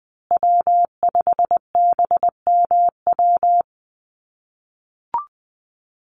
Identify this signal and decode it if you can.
Morse code, heard as a keyed tone, about 20 wpm, 700 Hz. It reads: W5BMW
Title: USA Call Signs Repeated 2x 20wpm